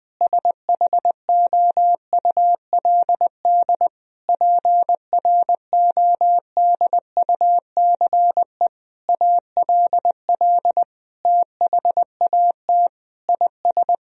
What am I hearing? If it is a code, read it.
Morse code, heard as a keyed tone, about 21 wpm, 700 Hz. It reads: SHOULD PRODUCE ALL THAT IS
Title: Sentences from Top 1000 Words - Review Part 3 - 20wpm